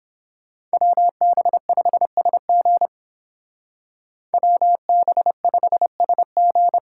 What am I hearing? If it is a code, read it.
Morse code, heard as a keyed tone, about 30 wpm, 700 Hz. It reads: WB5SG WB5SG